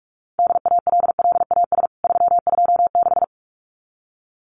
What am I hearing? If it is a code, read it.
Morse code, heard as a keyed tone, about 44 wpm, 700 Hz. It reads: DALLAS 326
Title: CWT Exchange 45wpm